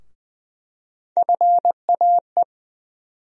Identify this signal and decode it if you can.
Morse code, heard as a keyed tone, about 20 wpm, 700 Hz. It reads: FAE
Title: USA Call Sign Postfixes - 3 Letters 20wpm